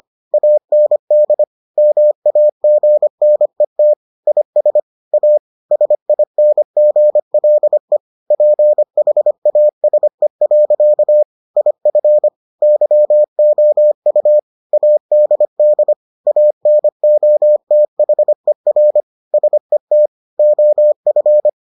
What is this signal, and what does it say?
Morse code, heard as a keyed tone, about 25 wpm, 600 Hz. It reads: AND MAGNET IS A SINGLE PHASE. IF YOU ADD ANOTHER SET OF